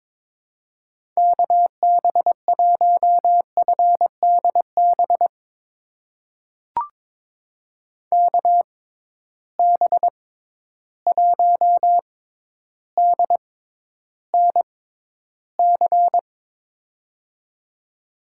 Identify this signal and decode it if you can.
Morse code, heard as a keyed tone, about 22 wpm, spaced as for 11 wpm, 700 Hz. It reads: KB1FDB KB1DNC
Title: USA Call Signs - 6 Characters - 6x Character Spacing - 22wpm